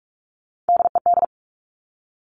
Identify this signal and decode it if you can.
Morse code, heard as a keyed tone, about 46 wpm, 700 Hz. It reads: DED
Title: Top Three Letter Combinations in Words 45wpm